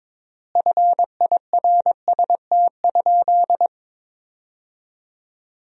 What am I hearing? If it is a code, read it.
Morse code, heard as a keyed tone, about 22 wpm, 700 Hz. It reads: FIRST?
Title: 3x Word-Spacing - Sentences from Top 100 Words - 22wpm